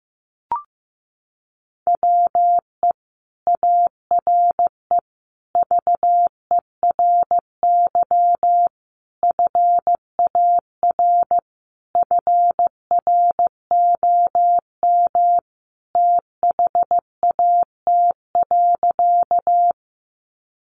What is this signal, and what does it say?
Morse code, heard as a keyed tone, about 15 wpm, 700 Hz. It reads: WE ARE VERY FAR FROM THAT.
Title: Sentences from Top 300 Words 15wpm